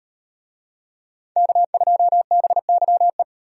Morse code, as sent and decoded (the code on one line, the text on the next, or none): -.- ..--- -... -.-- .
K2BYE